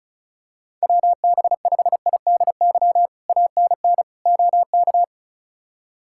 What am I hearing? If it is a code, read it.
Morse code, heard as a keyed tone, about 36 wpm, 700 Hz. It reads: WB5IDY ANN OK
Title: K1USN SST QSO Practice - 35wpm